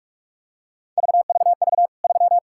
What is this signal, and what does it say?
Morse code, heard as a keyed tone, about 45 wpm, 700 Hz. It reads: VVV 3
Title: K1USN SST QSO Practice 45wpm